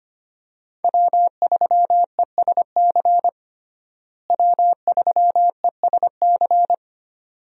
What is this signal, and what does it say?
Morse code, heard as a keyed tone, about 25 wpm, 690 Hz. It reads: W3ESC W3ESC
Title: USA Call Signs - 5 Characters - Repeated 2x - 25wpm